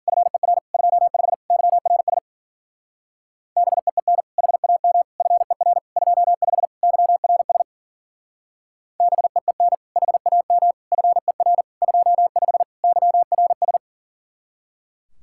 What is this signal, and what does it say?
Morse code, heard as a keyed tone, about 50 wpm, 700 Hz. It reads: FER 25 YRS BEEN HAM FER 25 YRS BEEN HAM FER 25 YRS
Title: Speed-Racing Common QSO Phrases - 40wpm